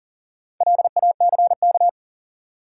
Morse code, heard as a keyed tone, about 40 wpm, 700 Hz. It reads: LACK
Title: Sets of 3 Words - New Word Drill - 40wpm